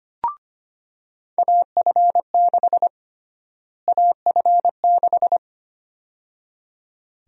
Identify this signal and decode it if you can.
Morse code, heard as a keyed tone, about 25 wpm, 700 Hz. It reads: AF6 AF6
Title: USA Call Sign Prefixes - Repeated 2x - 25wpm